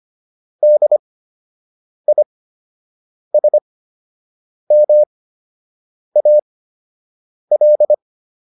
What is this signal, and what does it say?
Morse code, heard as a keyed tone, about 25 wpm, spaced as for 7 wpm, 600 Hz. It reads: DISMAL